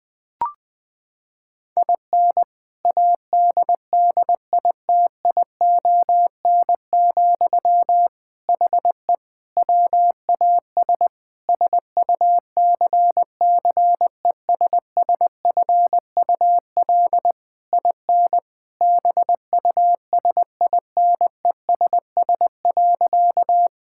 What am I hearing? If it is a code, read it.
Morse code, heard as a keyed tone, about 21 wpm, 700 Hz. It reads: IN ADDITION, HE WAS SUCCESSFUL IN BUSINESS.
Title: Sentences from Top 1000 Words - Review Part 6 - 20wpm